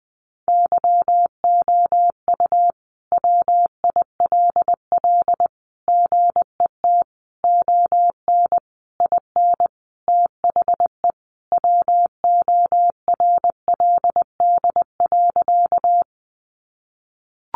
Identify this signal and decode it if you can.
Morse code, heard as a keyed tone, about 20 wpm, 700 Hz. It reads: YOU WILL GET ON IN THE WORLD.
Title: Sentences from Top 200 Words 20wpm